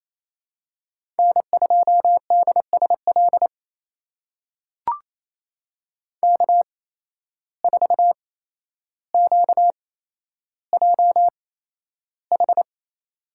Morse code, heard as a keyed tone, about 28 wpm, spaced as for 12 wpm, 700 Hz. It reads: N2DSL K4QJH
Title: USA Call Signs - 5 Characters - 8x Character Spacing - 28wpm